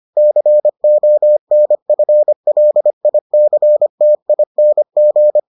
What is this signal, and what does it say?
Morse code, heard as a keyed tone, about 25 wpm, 600 Hz. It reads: CONFLICTING